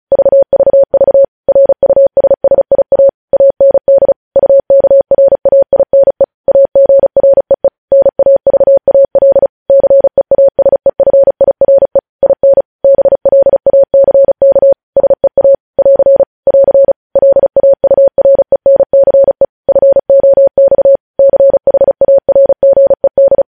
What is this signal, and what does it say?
Morse code, heard as a keyed tone, about 36 wpm, 550 Hz. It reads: VVV RUSSIA AND UKRAINE AGREE NAVAL CEASEFIRE IN BLACK SEA <AR> <AR> LAURENCE FOX CHARGED
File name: podcast / News headlines-1x-35x35